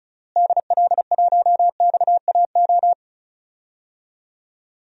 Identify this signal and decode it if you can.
Morse code, heard as a keyed tone, about 34 wpm, 700 Hz. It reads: DL1XAO
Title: DXCC-100 Call Signs - 6 Characters - Repeated 2x - 35wpm